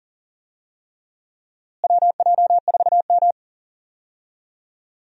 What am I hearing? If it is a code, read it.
Morse code, heard as a keyed tone, about 40 wpm, 700 Hz. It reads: WJ4M
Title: USA Call Signs - 4 Characters - Repeated 3x - 40wpm